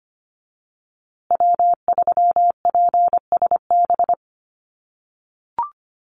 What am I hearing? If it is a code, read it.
Morse code, heard as a keyed tone, about 25 wpm, 700 Hz. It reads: W3PSB
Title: USA Call Signs Repeated 2x 25wpm